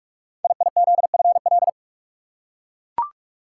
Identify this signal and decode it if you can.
Morse code, heard as a keyed tone, about 44 wpm, 690 Hz. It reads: IIZFL
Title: Code Groups - 5 Letters - 45wpm